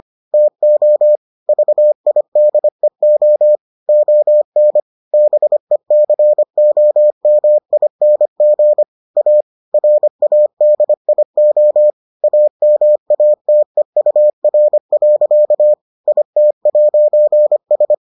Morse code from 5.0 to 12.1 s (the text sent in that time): BECOMING A RADIO